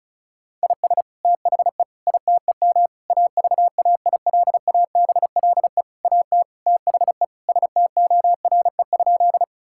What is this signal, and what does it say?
Morse code, heard as a keyed tone, about 36 wpm, 700 Hz. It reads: IS THE ITEM AVAILABLE AT THE STORE?
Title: Sentences from Top 1000 Words - Review Part 6 - 35wpm